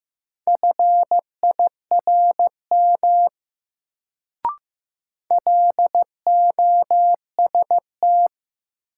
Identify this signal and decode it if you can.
Morse code, heard as a keyed tone, about 15 wpm, 700 Hz. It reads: FIRM LOST
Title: Sentences from Top 1400 Words - New Word Drill Alt - 15wpm